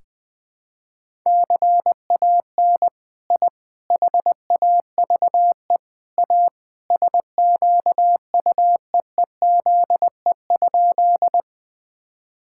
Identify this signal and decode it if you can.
Morse code, heard as a keyed tone, about 20 wpm, 700 Hz. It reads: CAN I HAVE A SQUEEZE?
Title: All-Age Appropriate Jokes - 20wpm